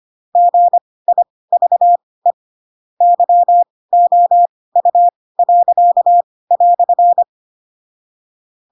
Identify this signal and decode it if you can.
Morse code, heard as a keyed tone, about 25 wpm, spaced as for 18 wpm, 700 Hz. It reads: GIVE YOU."
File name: Pinocchio / Pinocchio ch7 12 25 ik1hge